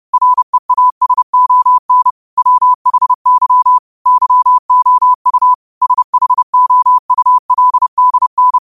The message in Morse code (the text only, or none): REASON WHY YOU SHOULDN